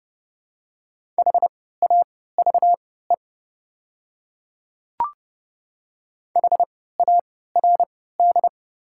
Morse code, heard as a keyed tone, about 30 wpm, spaced as for 17 wpm, 700 Hz. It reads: HAVE HARD
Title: 3x Character Spacing - Top 500 Words - 30wpm